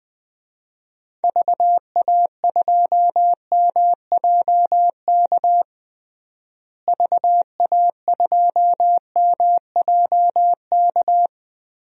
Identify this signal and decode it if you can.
Morse code, heard as a keyed tone, about 20 wpm, 700 Hz. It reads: VA2MJK VA2MJK
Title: DXCC-100 Call Signs - 6 Characters - Repeated 3x - 20wpm